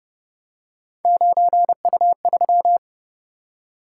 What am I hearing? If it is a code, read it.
Morse code, heard as a keyed tone, about 31 wpm, 700 Hz. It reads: 9U3
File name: Speed-Racing - Three Letter-Number Combinations - 25wpm